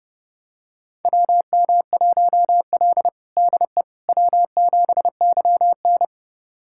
Morse code, heard as a keyed tone, about 30 wpm, 700 Hz. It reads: WM1L DE W7YN
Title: USA Callsign DE Callsign - 4 Characters - 30wpm